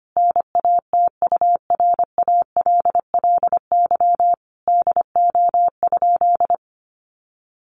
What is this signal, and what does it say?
Morse code, heard as a keyed tone, about 25 wpm, 700 Hz. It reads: NATURALLY DO?
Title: Sentences from Top 1300 Words 25wpm